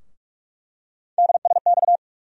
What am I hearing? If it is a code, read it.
Morse code, heard as a keyed tone, about 46 wpm, 700 Hz. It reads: DSX
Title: USA Call Sign Postfixes - 3 Letters 45wpm